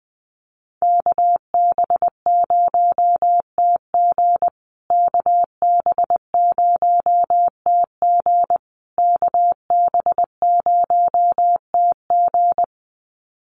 Morse code, heard as a keyed tone, about 20 wpm, 700 Hz. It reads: KB0TG KB0TG KB0TG
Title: USA Call Signs Repeated 3x 20wpm